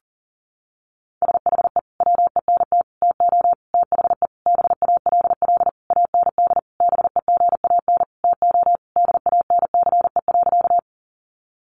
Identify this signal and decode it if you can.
Morse code, heard as a keyed tone, about 40 wpm, 700 Hz. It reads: SHE WENT TO THE BALL AND BEGAN TO DANCE.
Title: Sentences from Top 900 Words 40wpm